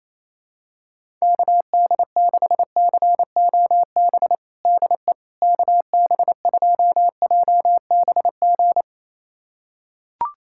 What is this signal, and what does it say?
Morse code, heard as a keyed tone, about 28 wpm, 700 Hz. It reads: KD6COB DE KB2JBG
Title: USA Callsign DE Callsign - 6 Characters - 28wpm